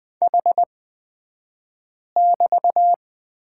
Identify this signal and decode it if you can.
Morse code, heard as a keyed tone, about 20 wpm, spaced as for 5 wpm, 700 Hz. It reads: H<BT>
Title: 254 Lesson 41d - Period - Two Characters 20wpm